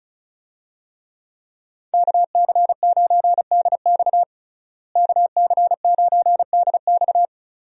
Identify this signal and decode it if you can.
Morse code, heard as a keyed tone, about 34 wpm, 700 Hz. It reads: KC9DX KC9DX
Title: USA Call Signs - 5 Characters - Repeated 2x - 35wpm